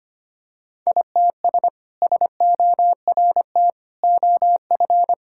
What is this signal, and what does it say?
Morse code, heard as a keyed tone, about 25 wpm, 700 Hz. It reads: ITS SORT OF
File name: practice / 25wpm / Sentences with 2-Word Phrases - 25wpm